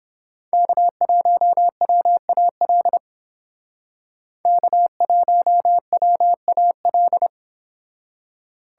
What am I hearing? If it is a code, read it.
Morse code, heard as a keyed tone, about 30 wpm, 700 Hz. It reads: K1WAL K1WAL